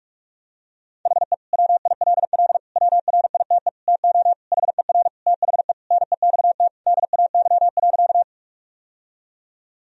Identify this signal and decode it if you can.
Morse code, heard as a keyed tone, about 46 wpm, 700 Hz. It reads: HE WILL WRITE TO HER THE NEXT DAY.
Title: Sentences from Top 1000 Words - Review Part 1 - 45wpm